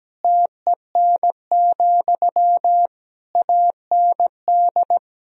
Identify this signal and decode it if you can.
Morse code, heard as a keyed tone, about 17 wpm, 700 Hz. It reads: TEN, AND
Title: Sentences from Top 1000 Words - Review Part 2 - 17wpm